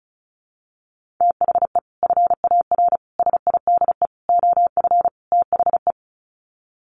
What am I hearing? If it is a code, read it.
Morse code, heard as a keyed tone, about 34 wpm, 700 Hz. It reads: THE FAR SIDE OF THE
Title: Sets of 5 Words 35wpm